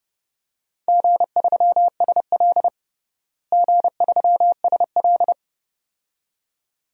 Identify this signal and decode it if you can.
Morse code, heard as a keyed tone, about 32 wpm, 700 Hz. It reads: G3SL G3SL
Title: DXCC-100 Call Signs - 4 Characters - Repeated 2x - 30wpm